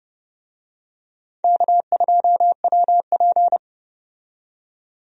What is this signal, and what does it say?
Morse code, heard as a keyed tone, about 31 wpm, 700 Hz. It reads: K2WP
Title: USA Call Signs - 4 Characters - 30wpm